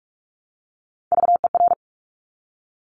Morse code, heard as a keyed tone, about 46 wpm, 700 Hz. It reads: VER
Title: Top Three Letter Combinations in Words 45wpm